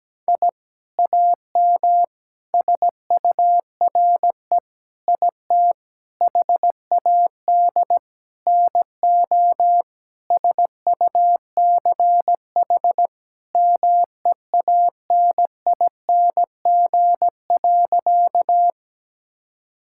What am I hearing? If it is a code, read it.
Morse code, heard as a keyed tone, about 17 wpm, 700 Hz. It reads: I AM SURE IT HAD NO SUCH MEANING.
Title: Sentences from Top 1000 Words - Review Part 6 - 17wpm